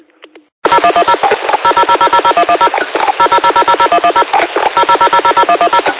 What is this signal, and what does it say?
DTMF, heard as telephone touch tones, tones of about 80 ms. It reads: 0110 000000110 000000110 000000110
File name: qsorec SimplexLogic